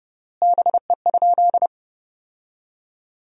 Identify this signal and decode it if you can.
Morse code, heard as a keyed tone, about 30 wpm, 700 Hz. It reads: BE?
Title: 3x Word-Spacing - Sentences from Top 100 Words - 30wpm